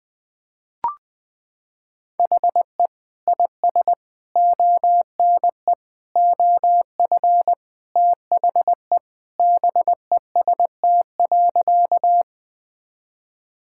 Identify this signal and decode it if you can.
Morse code, heard as a keyed tone, about 20 wpm, 700 Hz. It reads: HE IS ONE OF THE BEST.
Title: Sentences with 4-Word Phrases - 20wpm